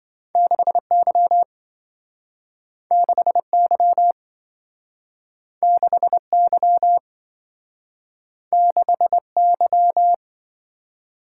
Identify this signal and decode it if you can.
Morse code, heard as a keyed tone, about 27 wpm, 700 Hz. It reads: 6Y 6Y 6Y 6Y